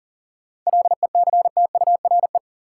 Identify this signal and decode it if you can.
Morse code, heard as a keyed tone, about 40 wpm, 700 Hz. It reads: LECTURE
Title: Sentences from Top 1600 Words - New Word Drill Alt - 40wpm